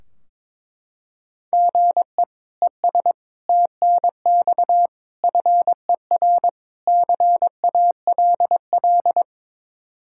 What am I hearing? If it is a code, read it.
Morse code, heard as a keyed tone, about 22 wpm, 700 Hz. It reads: GE ES TNX FER CALL
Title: K1USN SST QSO Practice 22wpm